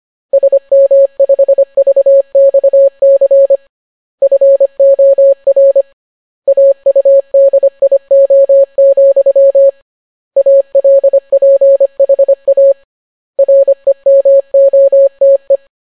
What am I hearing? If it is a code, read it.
Morse code, heard as a keyed tone, about 25 wpm, 550 Hz. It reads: SM5VXC FOR AUDIO, ALPHA REMOTE